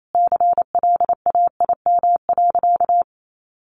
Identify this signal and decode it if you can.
Morse code, heard as a keyed tone, about 28 wpm, 700 Hz. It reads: CLAIM.
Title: Sentences from Top 1500 Words 28wpm